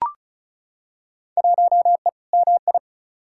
Morse code, heard as a keyed tone, about 36 wpm, 700 Hz. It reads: 1E MI
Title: Field Day Exchanges - 35wpm